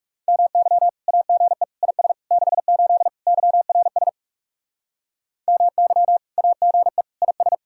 Morse code, heard as a keyed tone, about 36 wpm, 700 Hz. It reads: MY AGE IS 68 YRS MY AGE IS
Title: Speed-Racing Common QSO Phrases - 30wpm